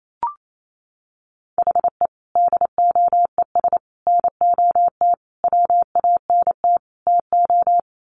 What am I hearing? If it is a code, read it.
Morse code, heard as a keyed tone, about 28 wpm, 700 Hz. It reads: HE DOES NOT WANT TO
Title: Sets of 5 Words 28wpm